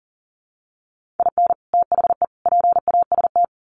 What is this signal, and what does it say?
Morse code, heard as a keyed tone, about 40 wpm, 700 Hz. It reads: IN THE PAST
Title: Sets of 3 Words 40wpm